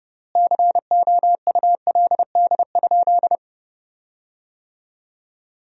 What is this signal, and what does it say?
Morse code, heard as a keyed tone, about 30 wpm, 700 Hz. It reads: COULD?
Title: 3x Word-Spacing - Sentences from Top 400 Words - 30wpm